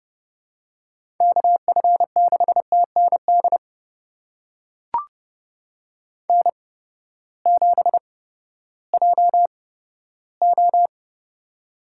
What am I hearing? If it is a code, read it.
Morse code, heard as a keyed tone, about 30 wpm, spaced as for 15 wpm, 700 Hz. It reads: KF6TND N7JO
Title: USA Call Signs - 8x Character Spacing - 30wpm